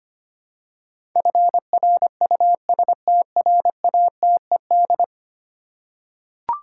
Morse code, heard as a keyed tone, about 25 wpm, 700 Hz. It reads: FRUSTRATED
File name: 10 Letter Words - First 3 Letters Given - 25wpm